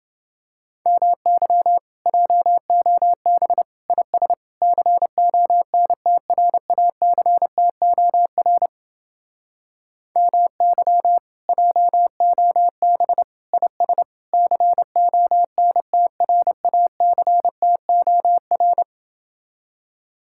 Morse code, heard as a keyed tone, about 28 wpm, 700 Hz. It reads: MY JOB IS CONTRACTOR MY JOB IS CONTRACTOR